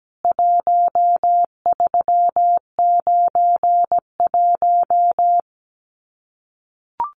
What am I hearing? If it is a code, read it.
Morse code, heard as a keyed tone, about 17 wpm, 700 Hz. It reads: 1391
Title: CWT Exchange 17wpm